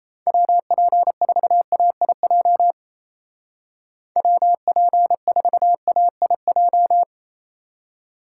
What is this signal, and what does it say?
Morse code, heard as a keyed tone, about 35 wpm, 700 Hz. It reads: WP4AIJ WP4AIJ